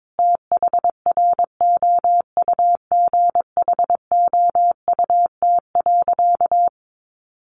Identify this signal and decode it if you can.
Morse code, heard as a keyed tone, about 22 wpm, 700 Hz. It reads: THROUGHOUT.
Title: Sentences from Top 800 Words 22wpm